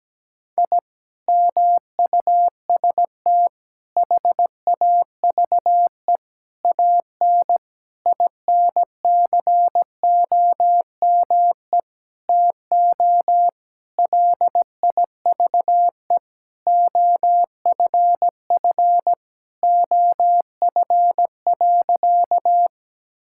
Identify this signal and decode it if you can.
Morse code, heard as a keyed tone, about 17 wpm, 700 Hz. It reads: I MUST HAVE AN INCOME TO LIVE OFF OF.
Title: Sentences from Top 1000 Words - Review Part 5 - 17wpm